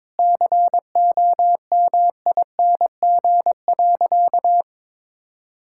Morse code, heard as a keyed tone, about 21 wpm, 700 Hz. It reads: COMING.
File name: practice / 22wpm / Sentences with 2-Word Phrases - 22wpm